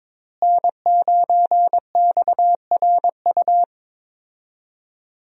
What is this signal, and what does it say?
Morse code, heard as a keyed tone, about 22 wpm, 700 Hz. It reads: N9XRU
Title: USA Call Signs - 5 Characters - Repeated 3x - 22wpm